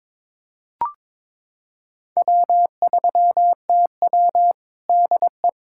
Morse code, heard as a keyed tone, about 21 wpm, 700 Hz. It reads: W3TW DE
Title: Practice QSO - Exchange 4 - Ending - 22wpm